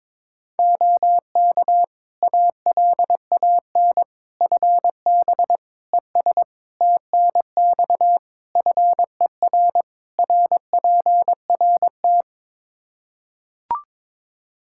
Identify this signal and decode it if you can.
Morse code, heard as a keyed tone, about 21 wpm, 700 Hz. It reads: OK ALAN FB ES TNX FER RPRT
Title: Practice QSO - Exchange 2 - Info - 22wpm